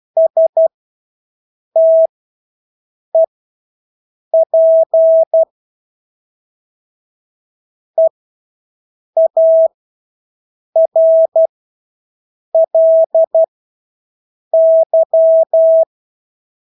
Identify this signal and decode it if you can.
Morse code, heard as a keyed tone, about 12 wpm, spaced as for 6 wpm, 650 Hz. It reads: STEP EARLY
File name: Engelske ord 12-6 wpm